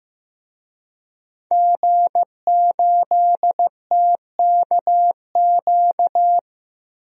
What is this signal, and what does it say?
Morse code, heard as a keyed tone, about 15 wpm, 700 Hz. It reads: G8TKQ
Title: DXCC-100 Call Signs - 5 Characters - Repeated 2x - 15wpm